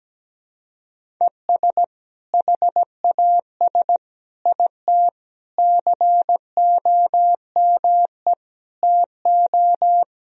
Morse code, heard as a keyed tone, about 17 wpm, 700 Hz. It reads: ES HAS IT COME TO
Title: Sentences from Top 100 Words and Abbreviations 17wpm